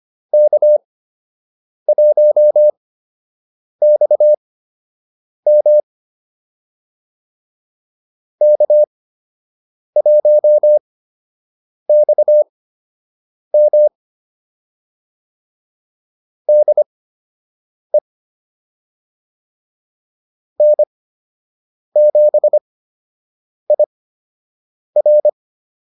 Morse code, heard as a keyed tone, about 25 wpm, spaced as for 7 wpm, 600 Hz. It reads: K1XM K1XM DE N7IR